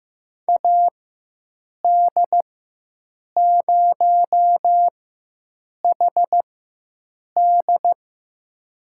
Morse code, heard as a keyed tone, about 15 wpm, spaced as for 7 wpm, 700 Hz. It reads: AD0HD
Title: USA Call Signs - 5 Characters - 4x Character Spacing - 15wpm